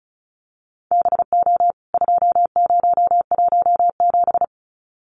Morse code, heard as a keyed tone, about 34 wpm, 700 Hz. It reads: BO 2017
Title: CWT Exchange 35wpm